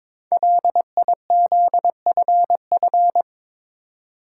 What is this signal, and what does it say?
Morse code, heard as a keyed tone, about 23 wpm, 690 Hz. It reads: LIZFF
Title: Code Groups - 5 Letters - 22wpm